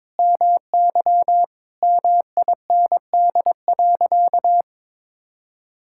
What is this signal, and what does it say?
Morse code, heard as a keyed tone, about 22 wpm, 700 Hz. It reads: MY MIND.
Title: Sentences from Top 1000 Words - Review Part 6 - 22wpm